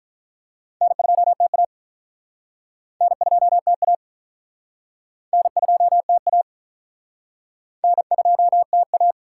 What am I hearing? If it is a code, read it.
Morse code, heard as a keyed tone, about 48 wpm, 700 Hz. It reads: N2TA N2TA N2TA N2TA